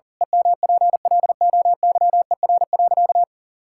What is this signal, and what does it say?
Morse code, heard as a keyed tone, about 37 wpm, 700 Hz. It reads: EMPLOYER.
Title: Pangram Challenge v2 - 40wpm